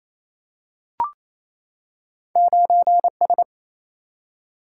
Morse code, heard as a keyed tone, about 28 wpm, 700 Hz. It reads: 9S
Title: Single-Letter Single-Number 28wpm